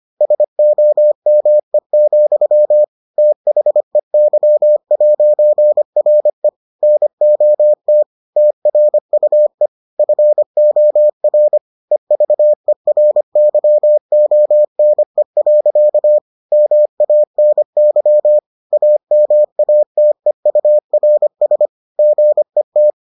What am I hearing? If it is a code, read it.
Morse code, heard as a keyed tone, about 25 wpm, 600 Hz. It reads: SOME, THEY'RE NOT TRUE FOR EVERYONE. MANY AMATEURS GET